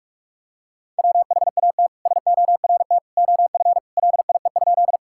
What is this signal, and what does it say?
Morse code, heard as a keyed tone, about 45 wpm, 700 Hz. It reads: WHAT SORT OF LIE?
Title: Sentences from Top 1600 Words 45wpm